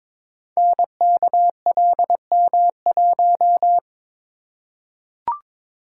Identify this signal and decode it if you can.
Morse code, heard as a keyed tone, about 22 wpm, 700 Hz. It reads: NKLM1
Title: Code Groups - 5 Letter:Numbers - 22wpm